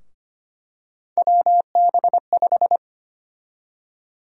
Morse code, heard as a keyed tone, about 25 wpm, 700 Hz. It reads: WB5
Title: USA Call Sign Prefixes - 25wpm